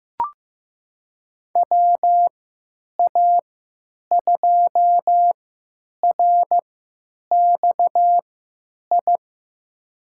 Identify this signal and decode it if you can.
Morse code, heard as a keyed tone, about 16 wpm, spaced as for 9 wpm, 700 Hz. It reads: WA2RXI
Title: USA Call Signs - 6 Characters - 3x Character Spacing - 15wpm